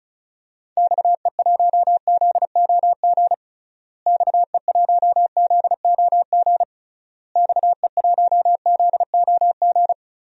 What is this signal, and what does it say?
Morse code, heard as a keyed tone, about 36 wpm, 700 Hz. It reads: XE1ZOG XE1ZOG XE1ZOG
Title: DXCC-100 Call Signs - 6 Characters - Repeated 3x - 35wpm